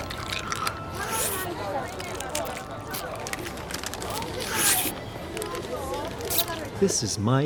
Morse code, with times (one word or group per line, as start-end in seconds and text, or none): none